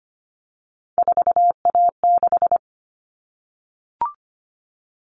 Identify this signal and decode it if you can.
Morse code, heard as a keyed tone, about 25 wpm, 700 Hz. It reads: 4A6
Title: All Cut Numbers - 3-Digit 25wpm